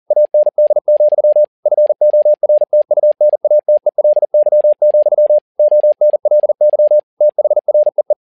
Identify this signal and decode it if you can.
Morse code, heard as a keyed tone, about 40 wpm, 600 Hz. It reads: AND, FORTUNATELY, ONLY THREE
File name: learning center / tower trouble / tower touble 40wpm